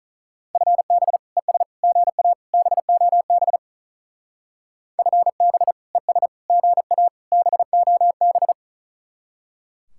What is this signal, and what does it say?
Morse code, heard as a keyed tone, about 43 wpm, 700 Hz. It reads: FB ES GA BOB FB ES GA BOB
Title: Speed-Racing Common QSO Phrases Encore - 35wpm